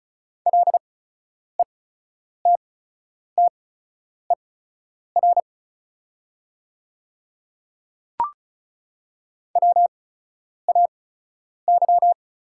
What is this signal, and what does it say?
Morse code, heard as a keyed tone, about 36 wpm, spaced as for 10 wpm, 700 Hz. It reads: LETTER WAY
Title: 8x Character Spacing - Top 300 Words - 35wpm